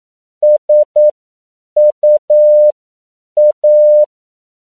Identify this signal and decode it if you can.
Morse code, heard as a keyed tone, about 9 wpm, spaced as for 7 wpm, 600 Hz. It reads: SUA
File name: CW / 10PPM